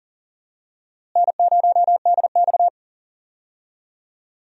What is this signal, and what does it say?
Morse code, heard as a keyed tone, about 40 wpm, 700 Hz. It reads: N0DX
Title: USA Call Signs - 4 Characters - Repeated 2x - 40wpm